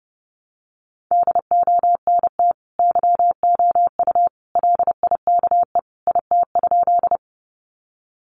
Morse code, heard as a keyed tone, about 30 wpm, 700 Hz. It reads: DONT YOU LIKE IT?
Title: Sentences from Top 300 Words 30wpm